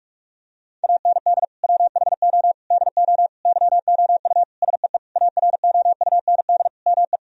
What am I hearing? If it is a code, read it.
Morse code, heard as a keyed tone, about 46 wpm, 700 Hz. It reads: AND WHO DO YOU SEE AROUND ME
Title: Sentences from Top 1000 Words - Review Part 3 - 45wpm